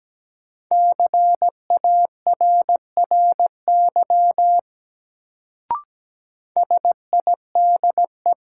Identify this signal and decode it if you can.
Morse code, heard as a keyed tone, about 17 wpm, 700 Hz. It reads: CARRY SIDE
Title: Sentences from Top 400 Words - New Word Drill - 17wpm